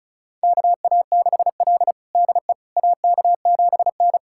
Morse code, heard as a keyed tone, about 34 wpm, 700 Hz. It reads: KA6L DE AK7N
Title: USA Callsign DE Callsign - 4 Characters - 35wpm